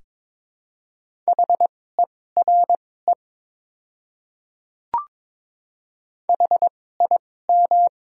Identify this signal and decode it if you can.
Morse code, heard as a keyed tone, about 22 wpm, spaced as for 16 wpm, 700 Hz. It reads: HERE HIM
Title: 2x Character Spacing - Top 200 Words - 22wpm